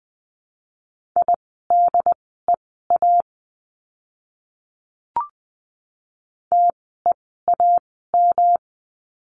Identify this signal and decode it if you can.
Morse code, heard as a keyed tone, about 20 wpm, spaced as for 14 wpm, 700 Hz. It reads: IDEA TEAM
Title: 108 Lesson 18f - Letter m - Words 20wpm